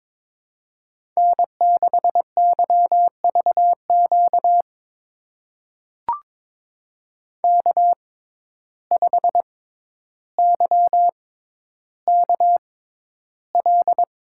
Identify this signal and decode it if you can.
Morse code, heard as a keyed tone, about 23 wpm, spaced as for 11 wpm, 700 Hz. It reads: N6YVQ K5YKL
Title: USA Call Signs - 5 Characters - 6x Character Spacing - 22wpm